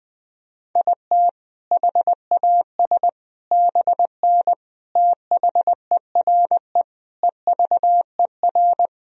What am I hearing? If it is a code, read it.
Morse code, heard as a keyed tone, about 20 wpm, 700 Hz. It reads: IT HAS BN THERE EVER
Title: Sentences from Top 400 Words and Abbreviations 20wpm